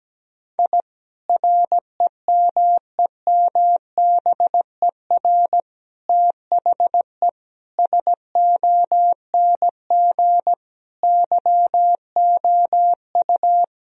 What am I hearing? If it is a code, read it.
Morse code, heard as a keyed tone, about 17 wpm, 700 Hz. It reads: I REMEMBER THE SONG YOU
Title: Sentences from Top 1000 Words - Review Part 2 - 17wpm